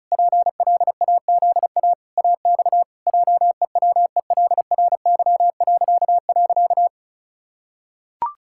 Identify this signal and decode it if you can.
Morse code, heard as a keyed tone, about 34 wpm, 700 Hz. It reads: PLAZA AX JEWELRY..
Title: Pangram Challenge v2 - 35wpm